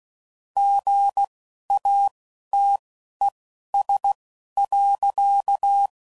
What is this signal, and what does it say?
Morse code, heard as a keyed tone, about 16 wpm, spaced as for 11 wpm, 800 Hz. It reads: GATES.
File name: CW Code Groups @10WPM / Technical Subjects / Gates Intro